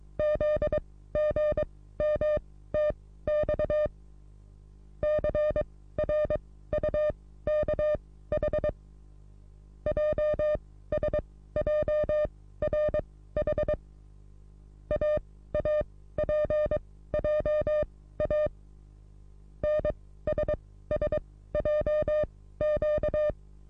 Morse code, heard as a keyed tone, about 22 wpm, spaced as for 15 wpm, 600 Hz. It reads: ZGMTX CRUKH JSJRH AAPJA NSSJQ
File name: repos / Morsekurs / DJ4UF Morsen27